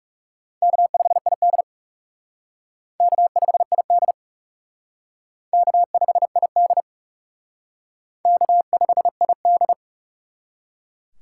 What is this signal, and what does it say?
Morse code, heard as a keyed tone, about 38 wpm, 700 Hz. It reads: K5ID K5ID K5ID K5ID